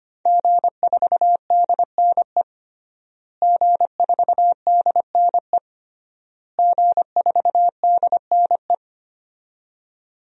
Morse code, heard as a keyed tone, about 25 wpm, 700 Hz. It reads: G4DNE G4DNE G4DNE
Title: DXCC-100 Call Signs - 5 Characters - Repeated 3x - 25wpm